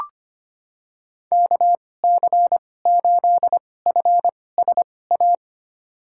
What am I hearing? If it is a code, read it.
Morse code, heard as a keyed tone, about 25 wpm, spaced as for 18 wpm, 700 Hz. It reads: KC8FSA
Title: USA Call Signs - 6 Characters - 2x Character Spacing - 25wpm